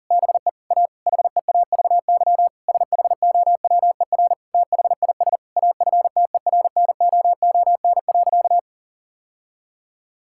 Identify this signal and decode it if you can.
Morse code, heard as a keyed tone, about 40 wpm, 700 Hz. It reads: BE A HEAVY SHOWER THIS AFTERNOON.
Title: Sentences from Top 1600 Words 40wpm